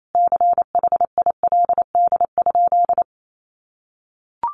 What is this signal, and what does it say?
Morse code, heard as a keyed tone, about 28 wpm, 700 Hz. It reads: CHILD?
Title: Sentences from Top 800 Words 28wpm